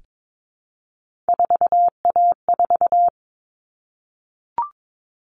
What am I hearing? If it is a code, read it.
Morse code, heard as a keyed tone, about 22 wpm, 700 Hz. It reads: 4A4
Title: All Cut Numbers - 3-Digit 22wpm